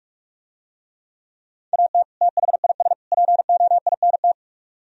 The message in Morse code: AT THIS POINT

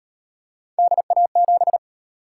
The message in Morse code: DA7

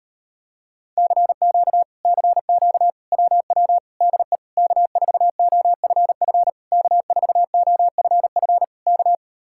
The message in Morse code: CQ CQ WW DE K4OFF K4OFF K

